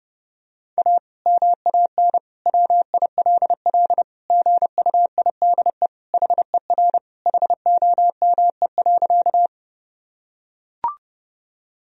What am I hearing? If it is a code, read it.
Morse code, heard as a keyed tone, about 32 wpm, 700 Hz. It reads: A MAN WILL GUIDE HER HOME.
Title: Sentences from Top 1000 Words - Review Part 3 - 30wpm